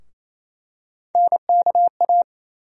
Morse code, heard as a keyed tone, about 28 wpm, 700 Hz. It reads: NKA
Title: USA Call Sign Postfixes - 3 Letters 28wpm